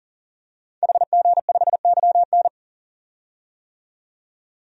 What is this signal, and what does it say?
Morse code, heard as a keyed tone, about 40 wpm, 700 Hz. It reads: HG5YN
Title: DXCC-100 Call Signs - 5 Characters - 40wpm